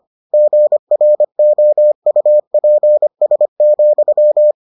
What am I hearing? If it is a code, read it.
Morse code, heard as a keyed tone, about 25 wpm, 600 Hz. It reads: GROUPS,